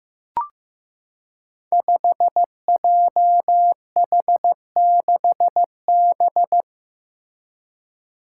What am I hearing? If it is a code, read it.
Morse code, heard as a keyed tone, about 15 wpm, 700 Hz. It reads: 5JH6B
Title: Five Challenging Letter-Number Combinations 15wpm